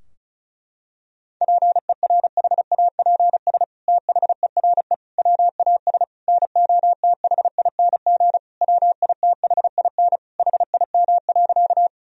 Morse code, heard as a keyed tone, about 36 wpm, 700 Hz. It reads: PERHAPS THERE WAS NOTHING WITHIN HIM.
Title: Sentences from Top 1000 Words - Review Part 4 - 35wpm